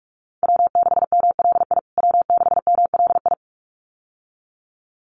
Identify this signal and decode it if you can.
Morse code, heard as a keyed tone, about 44 wpm, 700 Hz. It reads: W6MLI W6MLI
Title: USA Call Signs Repeated 2x 45wpm